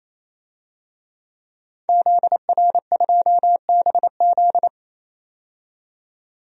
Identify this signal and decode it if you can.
Morse code, heard as a keyed tone, about 28 wpm, 700 Hz. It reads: ZR2BZ
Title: DXCC-100 Call Signs - 5 Characters - 28wpm